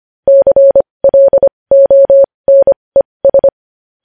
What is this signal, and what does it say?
Morse code, heard as a keyed tone, about 25 wpm, spaced as for 20 wpm, 550 Hz. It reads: CLONES
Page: news-headlines-in-morse-code-at-20-wpm